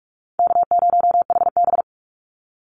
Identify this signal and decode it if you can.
Morse code, heard as a keyed tone, about 46 wpm, 700 Hz. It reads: K0HB
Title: Easy Call Signs - 45wpm